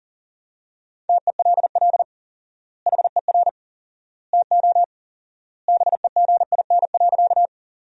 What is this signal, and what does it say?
Morse code, heard as a keyed tone, about 40 wpm, 700 Hz. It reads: TELL HER TO BEGIN.
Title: 4x Word-Spacing - Sentences from Top 300 Words - 40wpm